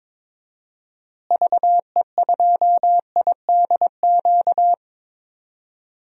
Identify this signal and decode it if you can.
Morse code, heard as a keyed tone, about 23 wpm, 700 Hz. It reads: VE2IDQ
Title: DXCC-100 Call Signs - 6 Characters - Repeated 2x - 22wpm